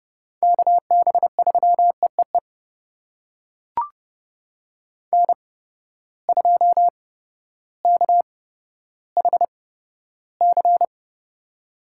Morse code, heard as a keyed tone, about 30 wpm, spaced as for 14 wpm, 700 Hz. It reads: KB3EEE N2KHC
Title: USA Call Signs - 8x Character Spacing - 30wpm